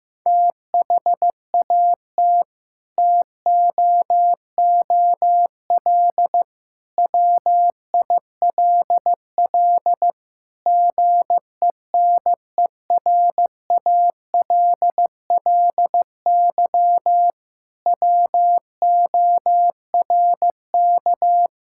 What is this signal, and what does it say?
Morse code, heard as a keyed tone, about 15 wpm, 700 Hz. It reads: THAT TOOL WILL GENERALLY WORK
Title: Sentences from Top 1000 Words - Review Part 6 - 15wpm